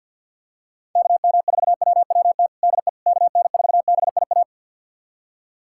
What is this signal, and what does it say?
Morse code, heard as a keyed tone, about 48 wpm, 700 Hz. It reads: KM4WWT DE KN4BIA
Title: USA Callsign DE Callsign 50wpm